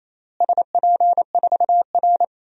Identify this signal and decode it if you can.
Morse code, heard as a keyed tone, about 28 wpm, 700 Hz. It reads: SP4R
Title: DXCC-100 Call Signs - 4 Characters - Repeated 2x - 28wpm